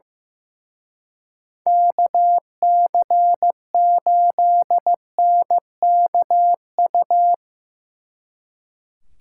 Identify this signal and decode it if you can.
Morse code, heard as a keyed tone, about 17 wpm, 700 Hz. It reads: KC8NKU
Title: Speed Racing - USA 2x3 Call Signs 15wpm